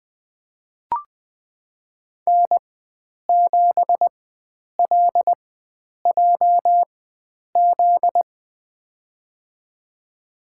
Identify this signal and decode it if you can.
Morse code, heard as a keyed tone, about 20 wpm, spaced as for 9 wpm, 700 Hz. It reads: N7LJZ
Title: USA Call Signs - 5 Characters - 4x Character Spacing - 20wpm